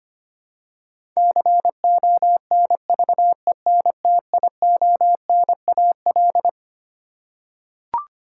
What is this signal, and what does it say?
Morse code, heard as a keyed tone, about 25 wpm, 700 Hz. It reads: CONVENTIONAL
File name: Twelve Letter Words - 25wpm